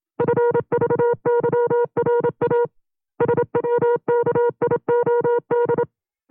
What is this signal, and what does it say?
Morse code, heard as a keyed tone, about 27 wpm, 480 Hz. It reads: FVYRA SWKIOD
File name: FreeFiles / CW-practice / 25WPM-side2